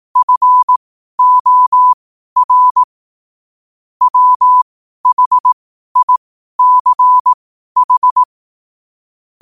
Morse code, heard as a keyed tone, about 18 wpm, spaced as for 12 wpm, 1000 Hz. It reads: FOR WHICH
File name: qotdfiles / 12wpm